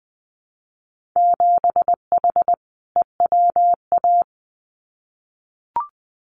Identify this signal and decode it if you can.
Morse code, heard as a keyed tone, about 20 wpm, 700 Hz. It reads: 7H EWA
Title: Winter Field Day Exchanges Encore 20wpm